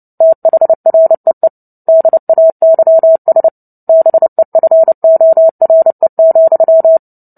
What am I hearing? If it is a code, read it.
Morse code, heard as a keyed tone, about 30 wpm, 650 Hz. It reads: THREE DAYS BEFORE,
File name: Book Two / 03-Chapter 3 - 29 WPM